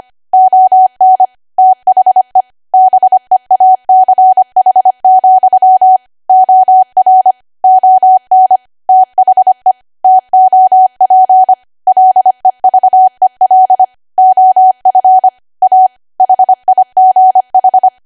Morse code, heard as a keyed tone, about 25 wpm, 750 Hz. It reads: ON THE BEACH, OR ON THE TOP LEVEL OF A HIGH
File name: Archive / 25 WPM / 25WPM